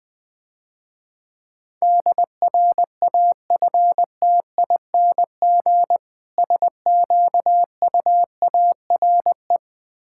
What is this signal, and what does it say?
Morse code, heard as a keyed tone, about 20 wpm, 700 Hz. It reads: DRAFTING SQUARE
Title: Difficult Words - Speed-Racing with Hints - 20wpm